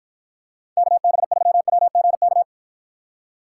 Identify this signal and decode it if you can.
Morse code, heard as a keyed tone, about 53 wpm, 700 Hz. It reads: KB3WGK